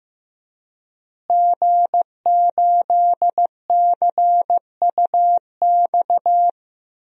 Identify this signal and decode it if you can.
Morse code, heard as a keyed tone, about 15 wpm, 700 Hz. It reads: G8CUX
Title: DXCC-100 Call Signs - 5 Characters - Repeated 2x - 15wpm